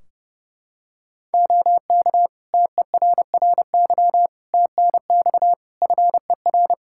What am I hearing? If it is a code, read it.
Morse code, heard as a keyed tone, about 29 wpm, 700 Hz. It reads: OK TERRY TNX FER
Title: Practice QSO - Exchange 4 - Ending - 30wpm